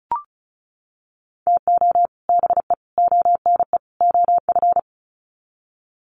Morse code, heard as a keyed tone, about 34 wpm, 700 Hz. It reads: TO BE ONE OF
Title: Sets of 4 Words 35wpm